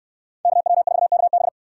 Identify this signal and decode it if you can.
Morse code, heard as a keyed tone, about 69 wpm, 700 Hz. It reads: KW4KB